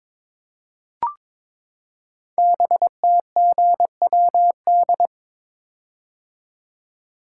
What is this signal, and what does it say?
Morse code, heard as a keyed tone, about 23 wpm, 700 Hz. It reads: BTGWD
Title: Code Groups - 5 Letters - 22wpm